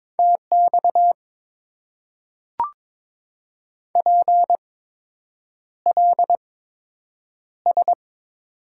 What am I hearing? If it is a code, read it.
Morse code, heard as a keyed tone, about 22 wpm, spaced as for 8 wpm, 700 Hz. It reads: TX PLS
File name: Top 90 QSO Elements - 8x Character Spacing - 22wpm